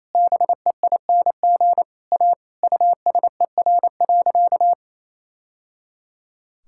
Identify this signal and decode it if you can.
Morse code, heard as a keyed tone, about 27 wpm, 700 Hz. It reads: BEING A USER.
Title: Sentences from Top 1000 Words - Review Part 5 - 28wpm